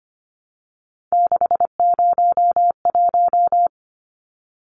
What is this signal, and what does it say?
Morse code, heard as a keyed tone, about 25 wpm, 700 Hz. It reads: 601
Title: Numbers - 3-Digit 25wpm